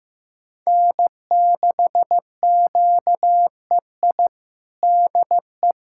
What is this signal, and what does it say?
Morse code, heard as a keyed tone, about 15 wpm, 700 Hz. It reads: N6QEI DE
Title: USA Callsign DE Callsign - 5 Characters - 15wpm